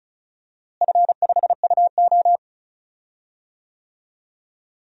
Morse code, heard as a keyed tone, about 34 wpm, 700 Hz. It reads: F5UO
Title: DXCC-100 Call Signs - 4 Characters - Repeated 2x - 35wpm